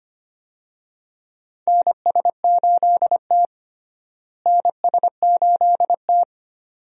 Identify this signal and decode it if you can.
Morse code, heard as a keyed tone, about 25 wpm, 700 Hz. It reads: NS8T NS8T
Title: USA Call Signs - 4 Characters - Repeated 2x - 25wpm